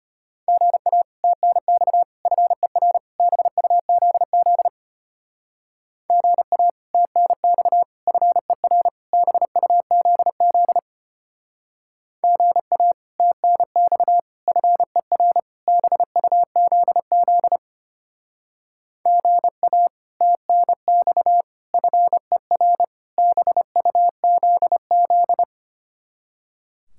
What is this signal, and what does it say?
Morse code, heard as a keyed tone, about 30 wpm, 700 Hz. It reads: GA TNX FER BUZZ GA TNX FER BUZZ GA TNX FER BUZZ GA TNX FER BUZZ
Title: Speed-Racing Common QSO Phrases - 25wpm